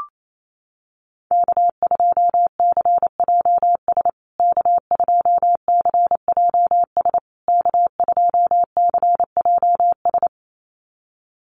Morse code, heard as a keyed tone, about 28 wpm, 700 Hz. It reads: K2CJS K2CJS K2CJS
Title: USA Call Signs Repeated 3x 28wpm